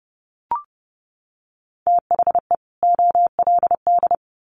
Morse code, heard as a keyed tone, about 30 wpm, 700 Hz. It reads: THE OLD
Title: Sets of 2 Words 30wpm